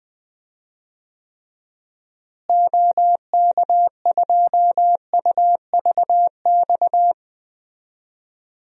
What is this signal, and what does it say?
Morse code, heard as a keyed tone, about 20 wpm, 700 Hz. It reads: OK2UVX
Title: DXCC-100 Call Signs - 6 Characters - Repeated 3x - 20wpm